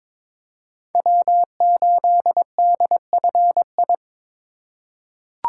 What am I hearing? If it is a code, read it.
Morse code, heard as a keyed tone, about 21 wpm, 700 Hz. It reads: W8DFI
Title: USA Call Signs - 5 Characters - 8x Character Spacing - 22wpm